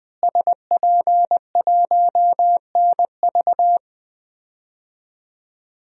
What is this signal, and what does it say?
Morse code, heard as a keyed tone, about 20 wpm, 700 Hz. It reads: SP1NV
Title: DXCC-100 Call Signs - 5 Characters - Repeated 3x - 20wpm